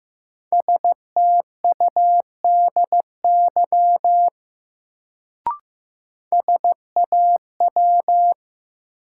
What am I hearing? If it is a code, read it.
Morse code, heard as a keyed tone, about 15 wpm, 700 Hz. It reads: STUDY SAW
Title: Sentences from Top 300 Words - New Word Drill - 15wpm